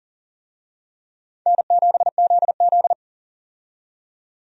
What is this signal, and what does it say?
Morse code, heard as a keyed tone, about 40 wpm, 700 Hz. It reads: N7ZZ
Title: USA Call Signs - 4 Characters - 40wpm